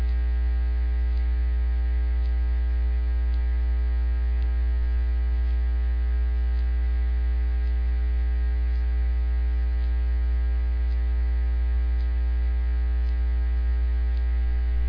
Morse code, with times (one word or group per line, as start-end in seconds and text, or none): none